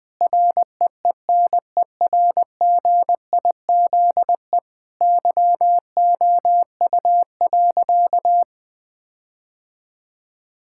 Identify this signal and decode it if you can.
Morse code, heard as a keyed tone, about 20 wpm, 700 Hz. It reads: REENERGIZE YOU.
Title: Difficult Sentences with Hints - 20wpm